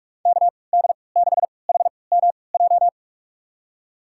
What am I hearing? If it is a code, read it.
Morse code, heard as a keyed tone, about 44 wpm, spaced as for 25 wpm, 700 Hz. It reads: KD6HMJ
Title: USA Call Signs - 6 Characters - 3x Character Spacing - 45wpm